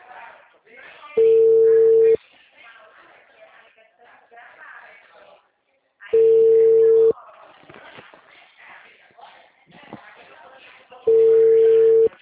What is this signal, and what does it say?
Ringback tone (China, 450 Hz), with speech.
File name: out- -2007- - - .3526